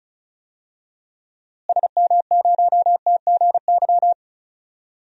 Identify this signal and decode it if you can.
Morse code, heard as a keyed tone, about 36 wpm, 700 Hz. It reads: SM0TGY
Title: DXCC-100 Call Signs - 6 Characters - Repeated 2x - 35wpm